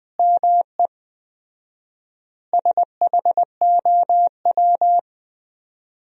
Morse code, heard as a keyed tone, about 20 wpm, 700 Hz. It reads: ME SHOW
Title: 4x Word-Spacing - Sentences from Top 300 Words - 20wpm